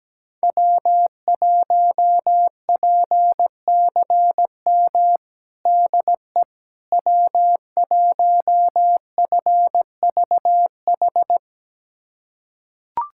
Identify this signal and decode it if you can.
Morse code, heard as a keyed tone, about 17 wpm, 700 Hz. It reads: W1PCM DE W1FVH
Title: USA Callsign DE Callsign - 5 Characters - 17wpm